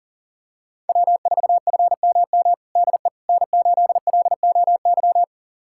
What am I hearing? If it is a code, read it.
Morse code, heard as a keyed tone, about 40 wpm, 700 Hz. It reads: W4FMM DE N8LOY
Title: USA Callsign DE Callsign - 5 Characters - 40wpm